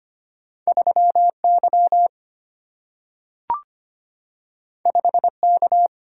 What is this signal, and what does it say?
Morse code, heard as a keyed tone, about 25 wpm, 700 Hz. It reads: 3 Y 5 K
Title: Single-Letter Single-Number 25wpm